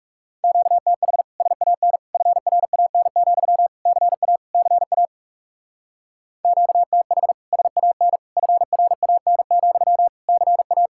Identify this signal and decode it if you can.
Morse code, heard as a keyed tone, about 35 wpm, 700 Hz. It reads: QTH SAN FRAN, CA CA QTH SAN FRAN, CA
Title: Speed-Racing Common QSO Phrases - 30wpm